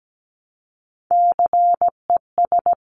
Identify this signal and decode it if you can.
Morse code, heard as a keyed tone, about 17 wpm, 700 Hz. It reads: CES
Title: Top Three Letter Combinations in Words 17wpm